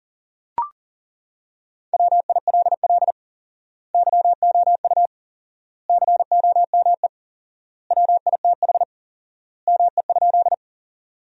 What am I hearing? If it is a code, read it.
Morse code, heard as a keyed tone, about 40 wpm, 700 Hz. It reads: WILL YOU COME WITH ME?
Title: 4x Word-Spacing - Sentences from Top 200 Words - 40wpm